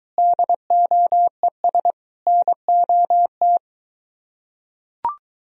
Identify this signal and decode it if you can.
Morse code, heard as a keyed tone, about 23 wpm, 700 Hz. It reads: DOES NOT